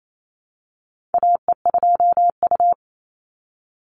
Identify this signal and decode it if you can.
Morse code, heard as a keyed tone, about 28 wpm, 700 Hz. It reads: AE2U
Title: Call Signs - 28wpm